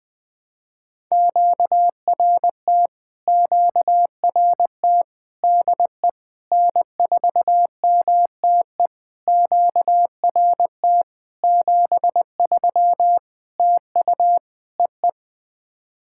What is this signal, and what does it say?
Morse code, heard as a keyed tone, about 20 wpm, 700 Hz. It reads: QRT QRT DE N4MTE QRT 73 TU EE
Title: POTA QSO Practice - 20wpm